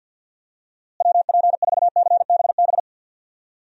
Morse code, heard as a keyed tone, about 52 wpm, 700 Hz. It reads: WP4CBB
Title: USA Call Signs - 6 Characters - Repeated 3x - 50wpm